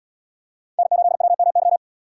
Morse code, heard as a keyed tone, about 77 wpm, 700 Hz. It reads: N8WMY